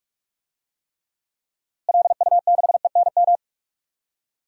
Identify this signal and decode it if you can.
Morse code, heard as a keyed tone, about 44 wpm, 700 Hz. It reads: LU6ENM
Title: DXCC-100 Call Signs - 6 Characters - Repeated 3x - 45wpm